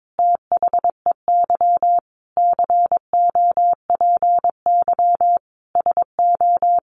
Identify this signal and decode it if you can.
Morse code, heard as a keyed tone, about 22 wpm, 700 Hz. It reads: THEY COPY SO
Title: Sentences from Top 1400 Words 22wpm